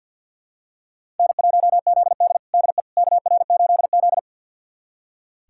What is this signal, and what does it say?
Morse code, heard as a keyed tone, about 47 wpm, 700 Hz. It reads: N1ZD DE KR8Z
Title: USA Callsign DE Callsign - 4 Characters - 50wpm